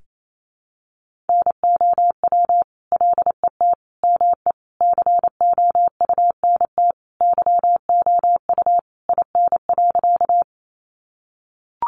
Morse code, heard as a keyed tone, about 28 wpm, 700 Hz. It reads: NOW LET ME COUNT YOU IN.
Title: Sentences from Top 1500 Words 28wpm